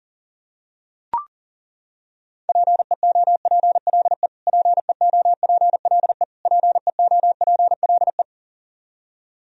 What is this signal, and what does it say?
Morse code, heard as a keyed tone, about 40 wpm, 700 Hz. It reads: PEOPLE PEOPLE PEOPLE
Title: Top 100 Words Repeated 40wpm